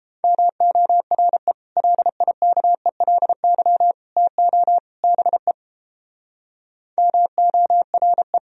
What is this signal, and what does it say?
Morse code, heard as a keyed tone, about 29 wpm, 700 Hz. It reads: MORE LIKELY TO BE MORE